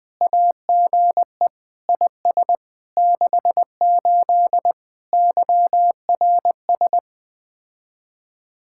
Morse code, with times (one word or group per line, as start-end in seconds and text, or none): 0.2-1.5 AGE
1.9-2.6 IS
3.0-4.7 68
5.1-7.0 YRS